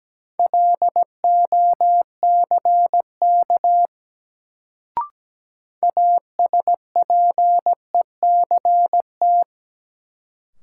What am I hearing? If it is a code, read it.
Morse code, heard as a keyed tone, about 17 wpm, 700 Hz. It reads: LOCK ASPECT
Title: Sentences from Top 1300 Words - New Word Drill Alt - 17wpm